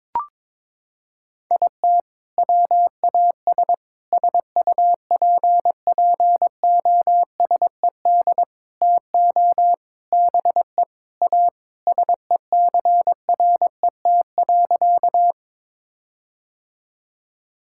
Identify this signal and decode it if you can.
Morse code, heard as a keyed tone, about 22 wpm, 700 Hz. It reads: IT WAS SUPPOSED TO BE A SECRET.
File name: practice / 22wpm / Sentences with 4-Word Phrases - 22wpm